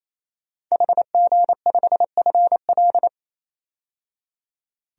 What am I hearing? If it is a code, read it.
Morse code, heard as a keyed tone, about 27 wpm, 700 Hz. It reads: HG5FL
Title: DXCC-100 Call Signs - 5 Characters - 28wpm